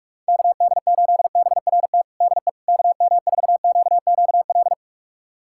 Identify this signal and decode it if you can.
Morse code, heard as a keyed tone, about 46 wpm, 700 Hz. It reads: KD8BRT DE KM4QQL
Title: USA Callsign DE Callsign 45wpm